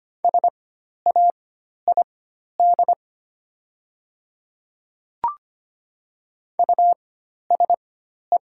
Morse code, heard as a keyed tone, about 25 wpm, spaced as for 12 wpm, 700 Hz. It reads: SAID USE
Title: 4x Character Spacing - Top 100 Words - 25wpm